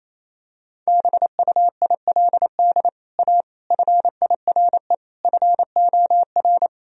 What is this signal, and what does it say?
Morse code, heard as a keyed tone, about 28 wpm, 700 Hz. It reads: BUILD A FIRE FOR
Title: Sentences from Top 1000 Words - Review Part 1 - 28wpm